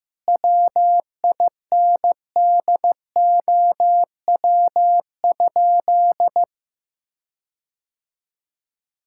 Morse code, heard as a keyed tone, about 16 wpm, 700 Hz. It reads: WINDOW?
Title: Sentences from Top 1600 Words 15wpm